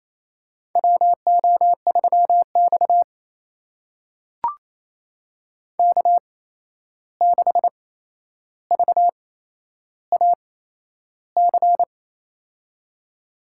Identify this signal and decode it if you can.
Morse code, heard as a keyed tone, about 28 wpm, spaced as for 11 wpm, 700 Hz. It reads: WO3X K6VAC
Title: USA Call Signs - 8x Character Spacing - 28wpm